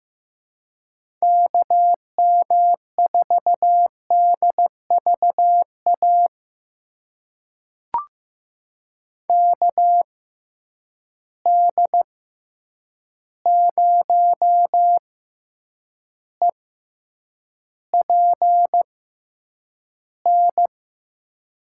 Morse code, heard as a keyed tone, about 15 wpm, spaced as for 8 wpm, 700 Hz. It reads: KM4DVA KD0EPN
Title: USA Call Signs - 6 Characters - 6x Character Spacing - 15wpm